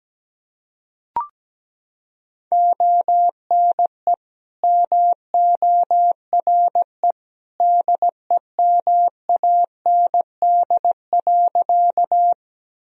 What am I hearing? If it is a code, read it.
Morse code, heard as a keyed tone, about 17 wpm, 700 Hz. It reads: ONE MORE DEMAND.
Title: Sentences from Top 1000 Words - Review Part 4 - 17wpm